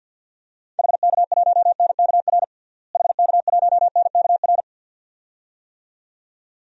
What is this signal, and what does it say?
Morse code, heard as a keyed tone, about 46 wpm, 700 Hz. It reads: HK1NKR HK1NKR
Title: DXCC-100 Call Signs - 6 Characters - Repeated 2x - 50wpm